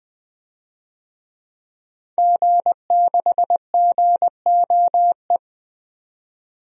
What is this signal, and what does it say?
Morse code, heard as a keyed tone, about 20 wpm, 700 Hz. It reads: G6GOE
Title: DXCC-100 Call Signs - 5 Characters - Repeated 2x - 20wpm